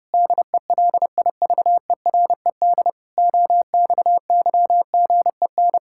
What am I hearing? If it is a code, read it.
Morse code, heard as a keyed tone, about 30 wpm, 700 Hz. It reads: DELIVERED OXYGEN
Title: Pangram Challenge v2 - 30wpm